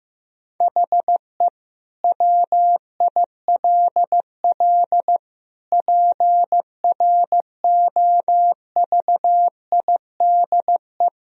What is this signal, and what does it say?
Morse code, heard as a keyed tone, about 15 wpm, 700 Hz. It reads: HE WILL PROVIDE
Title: Sentences from Top 1000 Words - Review Part 3 - 15wpm